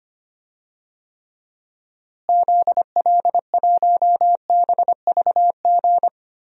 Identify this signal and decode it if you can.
Morse code, heard as a keyed tone, about 25 wpm, 700 Hz. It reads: ZL1BVG
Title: DXCC-100 Call Signs - 6 Characters - Repeated 3x - 25wpm